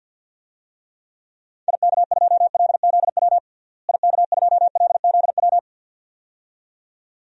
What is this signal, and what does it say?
Morse code, heard as a keyed tone, about 52 wpm, 700 Hz. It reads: IK2LZW IK2LZW
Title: DXCC-100 Call Signs - 6 Characters - Repeated 2x - 50wpm